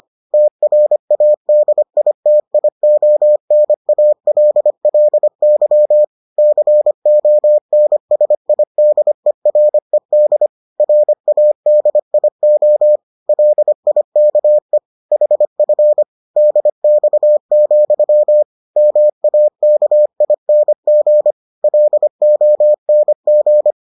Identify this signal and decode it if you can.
Morse code, heard as a keyed tone, about 25 wpm, 600 Hz. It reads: TRADITIONALLY CONSIDERED RADIO LIKE HF DX, MAKING LONG